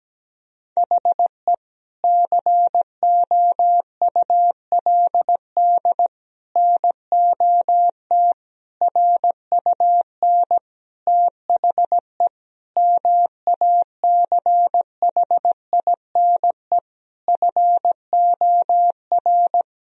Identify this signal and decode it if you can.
Morse code, heard as a keyed tone, about 17 wpm, 700 Hz. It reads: HE COULD NOT RUN THE MACHINE FOR
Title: Sentences from Top 1000 Words - Review Part 2 - 17wpm